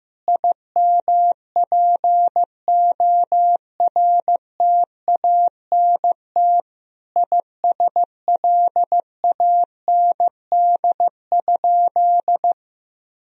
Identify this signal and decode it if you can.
Morse code, heard as a keyed tone, about 15 wpm, 700 Hz. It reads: IMPORTANT ISLAND?
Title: Sentences from Top 1000 Words - Review Part 3 - 15wpm